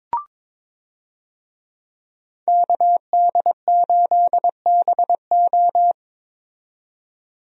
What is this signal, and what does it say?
Morse code, heard as a keyed tone, about 21 wpm, 700 Hz. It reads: KD8BO
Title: USA Call Signs - 5 Characters - 22wpm